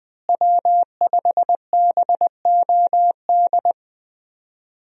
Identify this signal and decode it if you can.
Morse code, heard as a keyed tone, about 20 wpm, 700 Hz. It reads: W5BOD
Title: USA Call Signs - 5 Characters - Repeated 3x - 20wpm